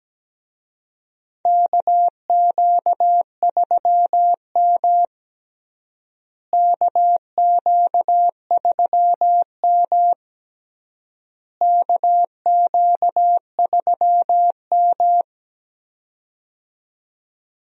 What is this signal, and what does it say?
Morse code, heard as a keyed tone, about 17 wpm, 700 Hz. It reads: KQ3M KQ3M KQ3M
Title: USA Call Signs - 4 Characters - Repeated 3x - 17wpm